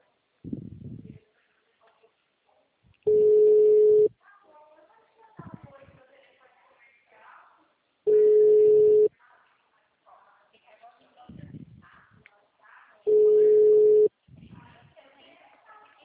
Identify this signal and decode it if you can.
Ringback tone (Europe, 425 Hz), with speech.
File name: out- -2038- - - .38644